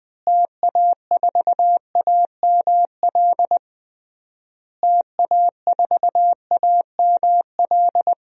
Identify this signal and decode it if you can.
Morse code, heard as a keyed tone, about 20 wpm, 700 Hz. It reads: TA4AML TA4AML
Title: DXCC-100 Call Signs - 6 Characters - Repeated 3x - 20wpm